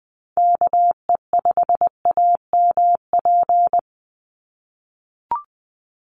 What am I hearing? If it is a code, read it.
Morse code, heard as a keyed tone, about 20 wpm, 700 Hz. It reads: KE5AMP
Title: USA Call Signs 20wpm